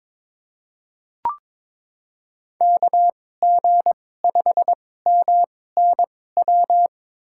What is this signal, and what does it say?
Morse code, heard as a keyed tone, about 22 wpm, spaced as for 16 wpm, 700 Hz. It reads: KG5MNW
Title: USA Call Signs - 6 Characters - 2x Character Spacing - 22wpm